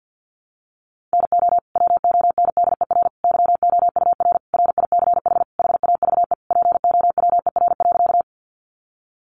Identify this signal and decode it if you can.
Morse code, heard as a keyed tone, about 50 wpm, 700 Hz. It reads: NO WONDER YOUR RICH HAVE POWER.
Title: Sentences from Top 1100 Words 50wpm